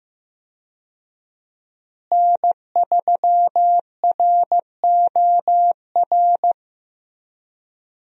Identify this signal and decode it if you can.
Morse code, heard as a keyed tone, about 15 wpm, 700 Hz. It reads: N3ROR
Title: USA Call Signs - 5 Characters - Repeated 2x - 15wpm